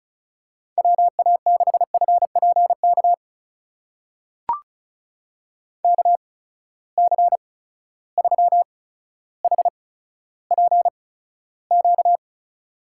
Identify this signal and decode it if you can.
Morse code, heard as a keyed tone, about 36 wpm, spaced as for 15 wpm, 700 Hz. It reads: WA6FPK KC3HPQ
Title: USA Call Signs - 6 Characters - 8x Character Spacing - 35wpm